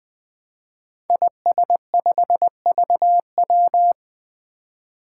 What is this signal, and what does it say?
Morse code, heard as a keyed tone, about 20 wpm, 700 Hz. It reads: IS5VW
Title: Five Challenging Letter-Number Combinations 20wpm